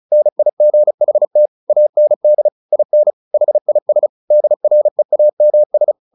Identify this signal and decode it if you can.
Morse code, heard as a keyed tone, about 34 wpm, 600 Hz. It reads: NIGHT AND IN HIS DREAMS
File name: the old man and the sea / the old man and the sea 1 35wpm